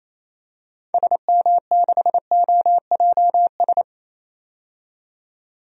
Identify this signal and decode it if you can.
Morse code, heard as a keyed tone, about 28 wpm, 700 Hz. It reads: SM6OJS
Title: DXCC-100 Call Signs - 6 Characters - 28wpm